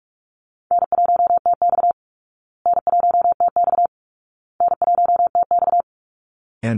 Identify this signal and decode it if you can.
Morse code, heard as a keyed tone, about 44 wpm, 700 Hz. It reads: N1TX N1TX N1TX
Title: Easy Call Signs Repeated 3x - 45wpm